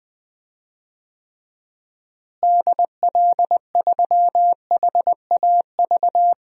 Morse code, heard as a keyed tone, about 20 wpm, 700 Hz. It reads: DL3HAV
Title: DXCC-100 Call Signs - 6 Characters - Repeated 2x - 20wpm